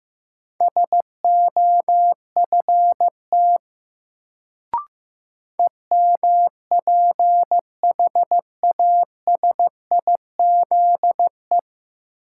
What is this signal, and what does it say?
Morse code, heard as a keyed tone, about 15 wpm, 700 Hz. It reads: SOFT EMPHASIZE
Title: Sentences from Top 1500 Words - New Word Drill - 15wpm